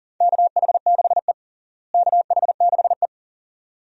Morse code, heard as a keyed tone, about 40 wpm, 700 Hz. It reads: KH6E KH6E
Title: DXCC-100 Call Signs - 4 Characters - Repeated 2x - 40wpm